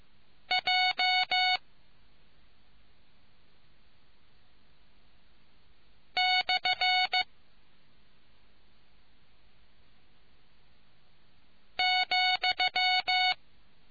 Morse code, heard as a keyed tone, about 15 wpm, 740 Hz. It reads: J / ,